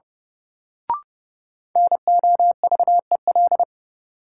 Morse code, heard as a keyed tone, about 29 wpm, 700 Hz. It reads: NOVEL
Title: Sentences from Top 1300 Words - New Word Drill Alt - 30wpm